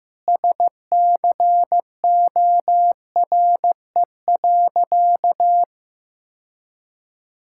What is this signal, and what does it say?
Morse code, heard as a keyed tone, about 15 wpm, 700 Hz. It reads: SCORE.
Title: Sentences from Top 1600 Words 15wpm